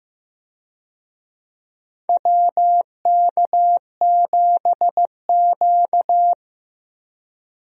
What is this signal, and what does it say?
Morse code, heard as a keyed tone, about 15 wpm, 700 Hz. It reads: WK7Q
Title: USA Call Signs - 4 Characters - Repeated 3x - 15wpm